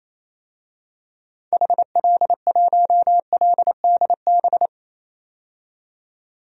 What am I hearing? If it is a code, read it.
Morse code, heard as a keyed tone, about 28 wpm, 700 Hz. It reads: HL1LDB
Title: DXCC-100 Call Signs - 6 Characters - 28wpm